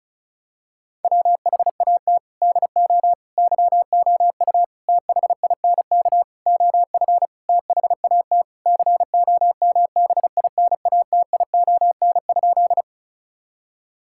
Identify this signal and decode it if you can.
Morse code, heard as a keyed tone, about 36 wpm, 700 Hz. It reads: WHAT DO YOU THINK OF THAT COMBINATION?
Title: Sentences from Top 1000 Words - Review Part 6 - 35wpm